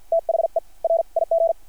Morse code, heard as a keyed tone, about 50 wpm, spaced as for 43 wpm, 660 Hz. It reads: THE RIG
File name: archived / CRSnet